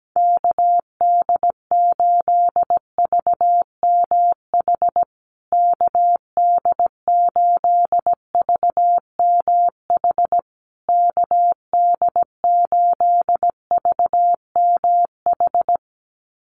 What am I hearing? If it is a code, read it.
Morse code, heard as a keyed tone, about 17 wpm, 700 Hz. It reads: KD8VMH KD8VMH KD8VMH
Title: USA Call Signs Repeated 3x 17wpm